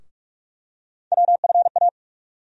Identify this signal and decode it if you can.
Morse code, heard as a keyed tone, about 42 wpm, 700 Hz. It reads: WFA
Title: USA Call Sign Postfixes - 3 Letters 45wpm